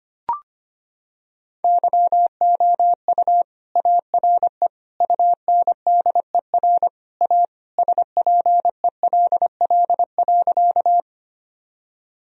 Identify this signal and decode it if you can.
Morse code, heard as a keyed tone, about 25 wpm, 700 Hz. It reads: YOU ARE UNDER A SPELL.
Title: Sentences from Top 1000 Words - Review Part 1 - 25wpm